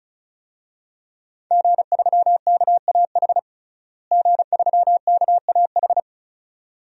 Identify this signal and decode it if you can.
Morse code, heard as a keyed tone, about 34 wpm, 700 Hz. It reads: G3KAH G3KAH
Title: DXCC-100 Call Signs - 5 Characters - Repeated 2x - 35wpm